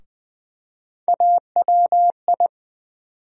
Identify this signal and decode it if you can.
Morse code, heard as a keyed tone, about 20 wpm, 700 Hz. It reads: AWI
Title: USA Call Sign Postfixes - 3 Letters 20wpm